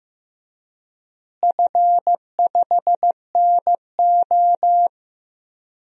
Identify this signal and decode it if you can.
Morse code, heard as a keyed tone, about 15 wpm, 700 Hz. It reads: F5NO
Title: DXCC-100 Call Signs - 4 Characters - Repeated 2x - 15wpm